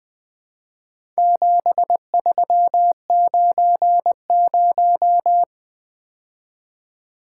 Morse code, heard as a keyed tone, about 23 wpm, 700 Hz. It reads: 7390
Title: Speed-Racing - Four Letter-Number Combinations - 20wpm